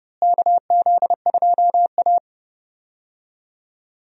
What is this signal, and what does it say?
Morse code, heard as a keyed tone, about 30 wpm, 700 Hz. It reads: KZ2A
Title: USA Call Signs - 4 Characters - Repeated 3x - 30wpm